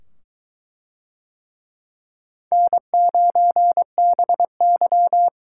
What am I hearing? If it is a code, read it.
Morse code, heard as a keyed tone, about 21 wpm, 700 Hz. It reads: N9BY